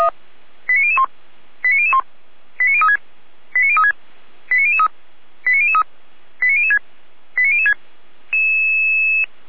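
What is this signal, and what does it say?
Five-tone selective call (ZVEI), heard as quick runs of five tones, tones of about 70 ms. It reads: 89001 89001 89036 89036 89003 89003 89006 89006
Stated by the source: Five-tone ZVEI selective call received on an SDR.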